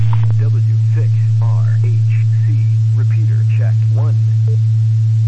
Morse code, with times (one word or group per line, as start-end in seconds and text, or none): none